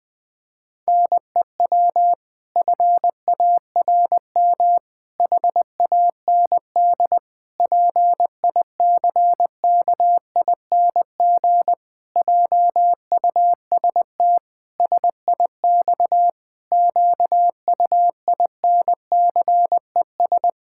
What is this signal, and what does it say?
Morse code, heard as a keyed tone, about 20 wpm, 700 Hz. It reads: NEW FARM HAND PICKING JUST SIX QUINCES
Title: Pangram Challenge v2 - 20wpm